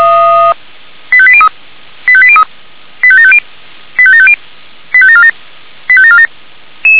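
Five-tone selective call (ZVEI), heard as quick runs of five tones, tones of about 70 ms. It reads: 85803 85803 85750 85750 85747 85747
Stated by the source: ZVEI five-tone call 85730, received by SDR.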